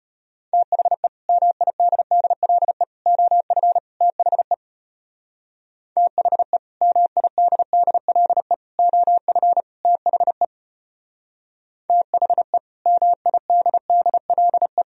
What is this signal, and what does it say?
Morse code, heard as a keyed tone, about 33 wpm, 700 Hz. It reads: THE MIDDLE OF THE THE MIDDLE OF THE THE MIDDLE